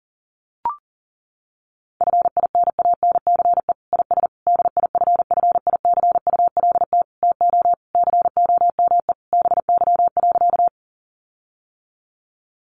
Morse code, heard as a keyed tone, about 40 wpm, 700 Hz. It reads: FINANCE IS DIFFICULT TO COME BY.
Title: Sentences from Top 1500 Words 40wpm